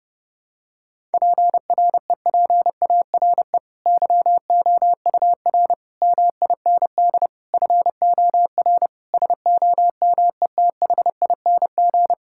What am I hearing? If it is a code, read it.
Morse code, heard as a keyed tone, about 30 wpm, 700 Hz. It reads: PREPARE YOUR MIND FOR SOMETHING
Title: Sentences from Top 1000 Words - Review Part 6 - 30wpm